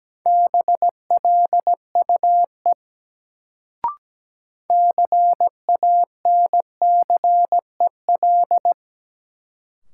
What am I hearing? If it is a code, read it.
Morse code, heard as a keyed tone, about 17 wpm, 700 Hz. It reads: BLUE CANCEL
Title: Sentences from Top 1200 Words - New Word Drill Alt - 17wpm